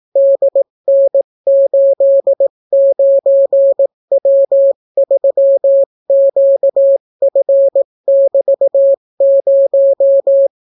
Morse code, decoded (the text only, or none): DN89W3QF<BT>0